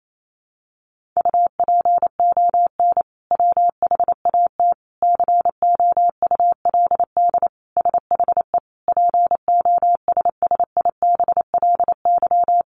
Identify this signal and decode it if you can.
Morse code, heard as a keyed tone, about 28 wpm, 700 Hz. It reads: UPON WHAT COULD SHE POSSIBLY
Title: Sentences from Top 1500 Words 28wpm